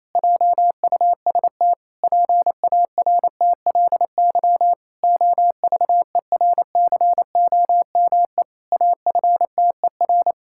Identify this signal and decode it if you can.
Morse code, heard as a keyed tone, about 28 wpm, 700 Hz. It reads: JUST PARTLY OVERCOME AFTER
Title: Pangram Challenge v2 - 28wpm